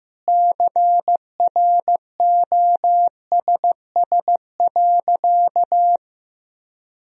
Morse code, heard as a keyed tone, about 15 wpm, 700 Hz. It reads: CROSS.
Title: Sentences from Top 1000 Words - Review Part 2 - 15wpm